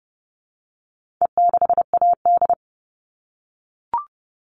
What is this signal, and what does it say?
Morse code, heard as a keyed tone, about 30 wpm, 700 Hz. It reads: E6AD
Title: All Cut Numbers - 4-Digit 30wpm